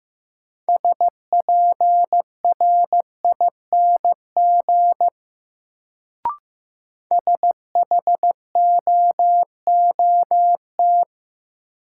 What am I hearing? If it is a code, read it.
Morse code, heard as a keyed tone, about 15 wpm, 700 Hz. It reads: SPRING SHOOT
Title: Sentences from Top 1200 Words - New Word Drill Alt - 15wpm